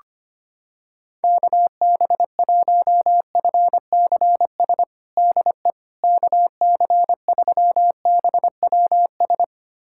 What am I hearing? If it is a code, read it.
Morse code, heard as a keyed tone, about 25 wpm, 700 Hz. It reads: KB1FCS DE KC3BWS
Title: USA Callsign DE Callsign 25wpm